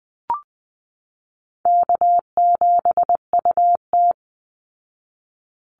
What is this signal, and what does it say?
Morse code, heard as a keyed tone, about 20 wpm, 700 Hz. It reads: K7UT
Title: Easy Call Signs Repeated 3x - 20wpm